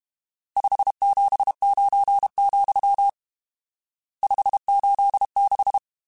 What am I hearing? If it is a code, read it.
Morse code, heard as a keyed tone, about 32 wpm, 800 Hz. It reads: 579, 586
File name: RSTs / RST-mp3s / Various RST @30